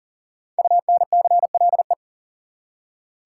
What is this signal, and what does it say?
Morse code, heard as a keyed tone, about 37 wpm, 700 Hz. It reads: UNCLE
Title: Five Letter Words - 40wpm